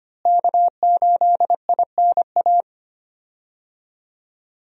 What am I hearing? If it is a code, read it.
Morse code, heard as a keyed tone, about 25 wpm, 700 Hz. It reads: K8INA
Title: USA Call Signs - 5 Characters - Repeated 2x - 25wpm